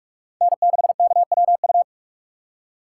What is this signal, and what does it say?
Morse code, heard as a keyed tone, about 46 wpm, 700 Hz. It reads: N6KWU
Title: USA Call Signs - 5 Characters - 4x Character Spacing - 45wpm